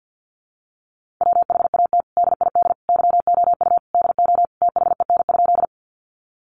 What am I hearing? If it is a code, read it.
Morse code, heard as a keyed tone, about 50 wpm, 700 Hz. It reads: WHAT DID YOU DO THEN?
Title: Sentences from Top 100 Words 50wpm